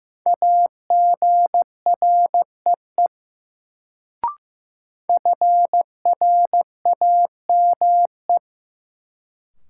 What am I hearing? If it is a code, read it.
Morse code, heard as a keyed tone, about 15 wpm, 700 Hz. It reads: AGREE FRAME
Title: Sentences from Top 1000 Words - Review Part 6 - New Word Drill - 15wpm